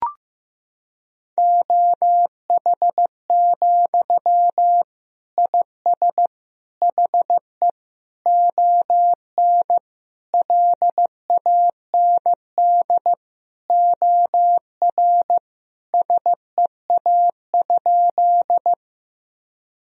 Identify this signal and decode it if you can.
Morse code, heard as a keyed tone, about 15 wpm, 700 Hz. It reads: OH, IS HE ON LAND OR SEA?
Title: Sentences from Top 1000 Words - Review Part 1 - 15wpm